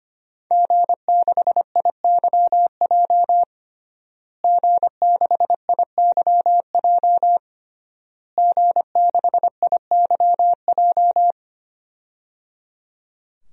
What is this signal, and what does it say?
Morse code, heard as a keyed tone, about 25 wpm, 700 Hz. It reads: G6IYJ G6IYJ G6IYJ
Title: DXCC-100 Call Signs - 5 Characters - Repeated 3x - 25wpm